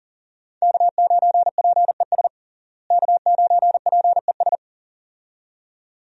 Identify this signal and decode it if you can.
Morse code, heard as a keyed tone, about 40 wpm, 700 Hz. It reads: K9PES K9PES
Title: USA Call Signs - 5 Characters - Repeated 2x - 40wpm